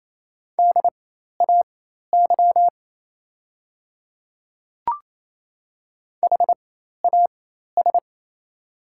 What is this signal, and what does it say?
Morse code, heard as a keyed tone, about 27 wpm, spaced as for 13 wpm, 690 Hz. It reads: DAY HAS
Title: 4x Character Spacing - Top 100 Words - 28wpm